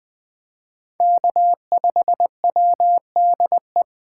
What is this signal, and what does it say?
Morse code, heard as a keyed tone, about 20 wpm, 700 Hz. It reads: K5WDE